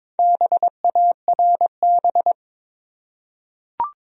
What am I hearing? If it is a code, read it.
Morse code, heard as a keyed tone, about 23 wpm, 700 Hz. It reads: BARB
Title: USA Common Names - Max 5 Letters - 22wpm